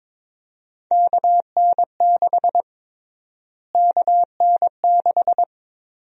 Morse code, heard as a keyed tone, about 22 wpm, 700 Hz. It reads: KN6 KN6
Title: USA Call Sign Prefixes - Repeated 2x - 22wpm